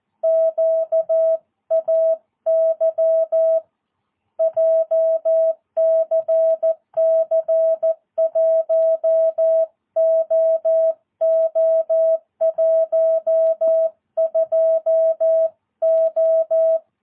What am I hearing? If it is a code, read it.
Morse code, heard as a keyed tone, about 14 wpm, 660 Hz. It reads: QAY JCC 1OO12O
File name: jh8cbh / 2022cw